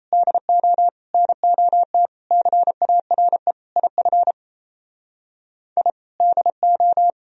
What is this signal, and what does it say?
Morse code, heard as a keyed tone, about 32 wpm, 700 Hz. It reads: DO NOT CARE IF I DO
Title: Speed-Racing - Sets of 5 Words 28wpm